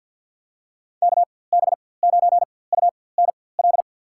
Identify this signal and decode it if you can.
Morse code, heard as a keyed tone, about 50 wpm, spaced as for 23 wpm, 700 Hz. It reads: KB9UNL
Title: USA Call Signs - 6 Characters - 4x Character Spacing - 50wpm